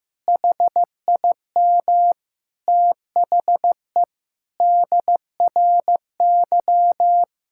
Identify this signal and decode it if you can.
Morse code, heard as a keyed tone, about 15 wpm, 700 Hz. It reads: HIM THE DRY
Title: Sentences from Top 1600 Words 15wpm